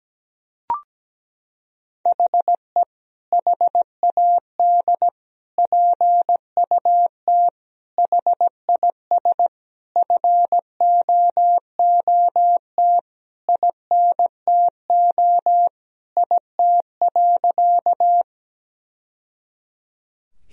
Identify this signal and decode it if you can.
Morse code, heard as a keyed tone, about 17 wpm, 700 Hz. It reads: HE HAD PUT HIS FOOT INTO IT.
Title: Sentences from Top 1000 Words - Review Part 3 - 17wpm